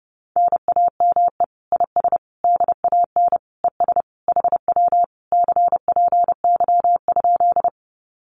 Morse code, heard as a keyed tone, about 30 wpm, 700 Hz. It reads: NAME IS DAN ES HW CPY?
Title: Common QSO Phrases Encore 30wpm